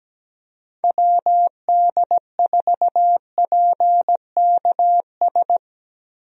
Morse code, heard as a keyed tone, about 17 wpm, 700 Hz. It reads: WD4PKS
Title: USA Call Signs - 6 Characters - Repeated 3x - 17wpm